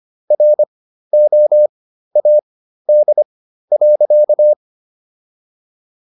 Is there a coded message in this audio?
Yes: Morse code from 0.3 to 4.5 s, heard as a keyed tone, about 25 wpm, spaced as for 13 wpm, 600 Hz. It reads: ROAD.